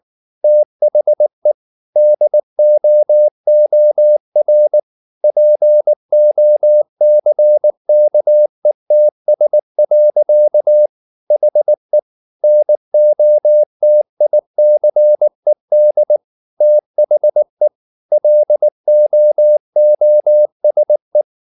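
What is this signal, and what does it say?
Morse code, heard as a keyed tone, about 19 wpm, 600 Hz. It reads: THE DOOR POCKETS. HE NOTICED THE LOOSE
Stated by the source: Morse code at 19 wpm from CW med Gnister, reading Ken Follett's novel Never in English.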